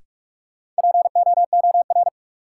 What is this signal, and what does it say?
Morse code, heard as a keyed tone, about 45 wpm, 700 Hz. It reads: POOR
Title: Sentences from Top 1000 Words - New Word Drill - 45wpm